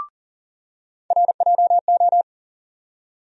Morse code, heard as a keyed tone, about 40 wpm, 700 Hz. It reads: RJO
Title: USA Call Sign Postfixes - 3 Letters 40wpm